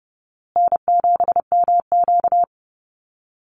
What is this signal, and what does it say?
Morse code, heard as a keyed tone, about 30 wpm, 700 Hz. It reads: N7MQ
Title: Easy Call Signs Repeated 3x - 30wpm